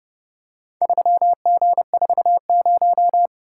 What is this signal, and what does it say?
Morse code, heard as a keyed tone, about 29 wpm, 700 Hz. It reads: 3G40